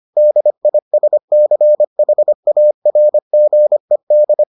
Morse code, heard as a keyed tone, about 25 wpm, 600 Hz. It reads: DISCHARGED